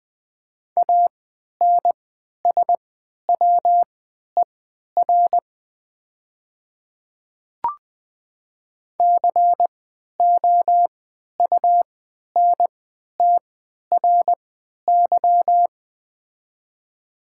Morse code, heard as a keyed tone, about 20 wpm, spaced as for 11 wpm, 700 Hz. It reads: ANSWER COUNTRY
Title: 3x Character Spacing - Top 300 Words - 20wpm